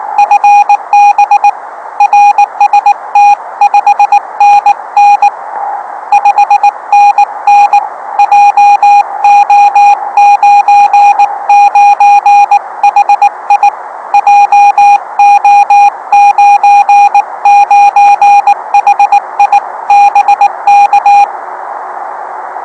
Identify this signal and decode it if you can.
Morse code, heard as a keyed tone, about 19 wpm, 820 Hz. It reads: FB RST 5NN 5NN JO99HI JO99HI BK